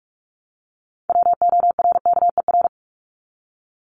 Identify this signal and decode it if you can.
Morse code, heard as a keyed tone, about 44 wpm, 700 Hz. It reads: WORKER
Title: Top 1751-2000 Words 45wpm